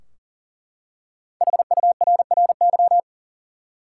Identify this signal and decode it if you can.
Morse code, heard as a keyed tone, about 40 wpm, 700 Hz. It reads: HURRY
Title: Five Letter Words - 40wpm